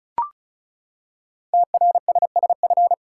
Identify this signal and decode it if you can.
Morse code, heard as a keyed tone, about 36 wpm, 700 Hz. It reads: TRSSF
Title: Code Groups - 5 Letters - 35wpm